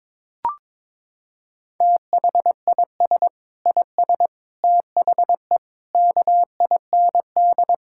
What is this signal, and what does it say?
Morse code, heard as a keyed tone, about 22 wpm, 700 Hz. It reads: THIS IS THE KIND
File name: Sentences with 7-Word Phrases - 22wpm